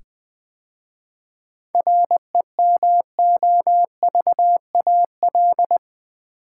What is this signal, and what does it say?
Morse code, heard as a keyed tone, about 20 wpm, 700 Hz. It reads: REMOVAL
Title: 7 Letter Words - First 3 Letters Given - 20wpm